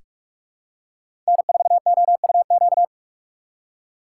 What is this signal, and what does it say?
Morse code, heard as a keyed tone, about 41 wpm, 700 Hz. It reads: N4OUQ
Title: USA Call Signs - 5 Characters - 4x Character Spacing - 45wpm